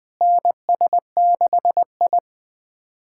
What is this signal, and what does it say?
Morse code, heard as a keyed tone, about 20 wpm, 700 Hz. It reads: NS6I
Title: SOTA QSO Practice 20wpm